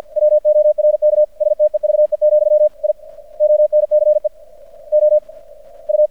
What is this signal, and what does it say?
Morse code, heard as a keyed tone, about 50 wpm, spaced as for 39 wpm, 600 Hz. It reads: WORK STEVE, I ONCE OU
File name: archived / CRSnet